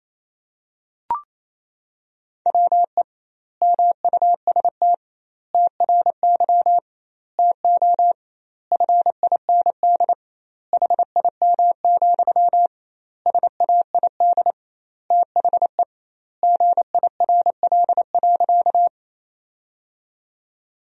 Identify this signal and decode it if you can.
Morse code, heard as a keyed tone, about 28 wpm, 700 Hz. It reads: WE MUST TRY TO FIND HIM, SAID THE GIRL.
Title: 2x Word-Spacing - Sentences from Top 400 Words - 28wpm